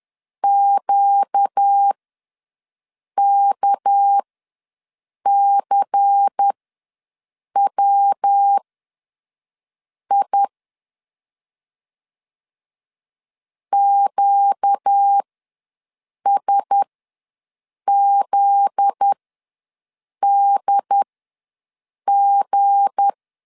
Morse code, heard as a keyed tone, about 10 wpm, spaced as for 6 wpm, 800 Hz. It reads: QKCWI QSZDG
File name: LEZIONI / 7 GF9Z / 7 GF9Z ws 10wpm 3919